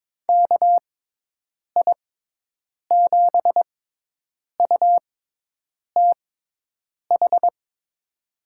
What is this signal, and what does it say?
Morse code, heard as a keyed tone, about 23 wpm, spaced as for 8 wpm, 700 Hz. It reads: KI7UTH
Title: USA Call Signs - 6 Characters - 6x Character Spacing - 22wpm